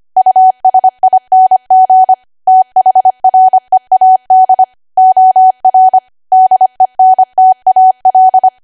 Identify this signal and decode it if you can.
Morse code, heard as a keyed tone, about 25 wpm, 750 Hz. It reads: USING THREAD OR DENTAL